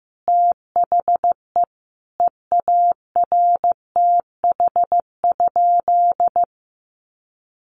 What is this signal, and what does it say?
Morse code, heard as a keyed tone, about 15 wpm, 700 Hz. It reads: THE EARTH?
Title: Sentences from Top 1200 Words 15wpm